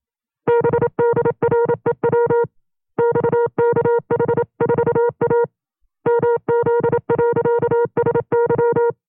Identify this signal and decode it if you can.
Morse code, heard as a keyed tone, about 27 wpm, 470 Hz. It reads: BDREW XKH4A MZ.SY
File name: FreeFiles / CW-practice / 25WPM-side1